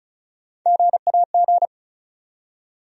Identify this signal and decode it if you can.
Morse code, heard as a keyed tone, about 36 wpm, 700 Hz. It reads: GAG
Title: USA Call Sign Postfixes - 3 Letters 35wpm